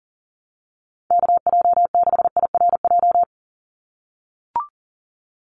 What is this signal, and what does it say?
Morse code, heard as a keyed tone, about 40 wpm, 700 Hz. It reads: KJ6IRJ
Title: USA Call Signs 40wpm